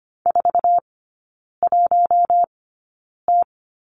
Morse code, heard as a keyed tone, about 25 wpm, spaced as for 9 wpm, 700 Hz. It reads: 41T
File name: Lesson 14e - Number 2 - Three Characters 25wpm